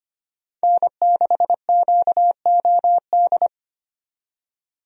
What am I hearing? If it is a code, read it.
Morse code, heard as a keyed tone, about 25 wpm, 700 Hz. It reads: N6QOD
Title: USA Call Signs - 5 Characters - 25wpm